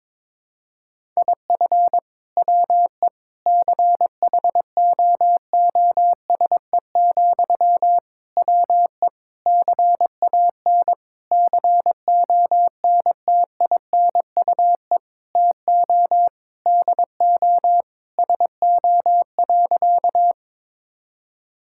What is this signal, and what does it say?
Morse code, heard as a keyed tone, about 22 wpm, 700 Hz. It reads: IF WE CHOOSE, WE CAN CONTINUE TO DO SO.
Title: Sentences from Top 1000 Words - Review Part 6 - 22wpm